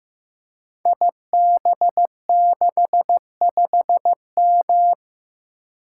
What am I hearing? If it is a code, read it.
Morse code, heard as a keyed tone, about 15 wpm, 700 Hz. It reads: IB65M
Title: Code Groups - 5 Letter:Numbers - 15wpm